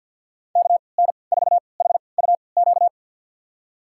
Morse code, heard as a keyed tone, about 52 wpm, spaced as for 29 wpm, 700 Hz. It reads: KN4HUQ
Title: USA Call Signs - 6 Characters - 3x Character Spacing - 50wpm